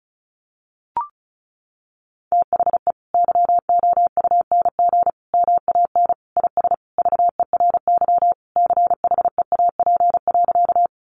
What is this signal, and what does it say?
Morse code, heard as a keyed tone, about 34 wpm, 700 Hz. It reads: THE YOUNG MAN IS VERY CHEAP.
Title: Sentences from Top 1200 Words 35wpm